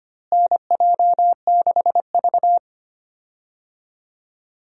Morse code, heard as a keyed tone, about 25 wpm, 700 Hz. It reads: NJ6V
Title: USA Call Signs - 4 Characters - Repeated 2x - 25wpm